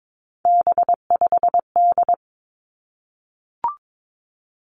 Morse code, heard as a keyed tone, about 22 wpm, 700 Hz. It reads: B5D
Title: Three Challenging Letter-Number Combinations 22wpm